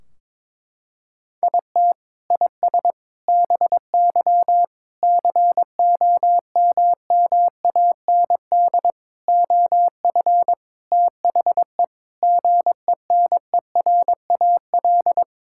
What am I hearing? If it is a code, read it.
Morse code, heard as a keyed tone, about 22 wpm, 700 Hz. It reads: IT IS BY COMMAND OF THE GENERAL
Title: Sentences from Top 1600 Words 22wpm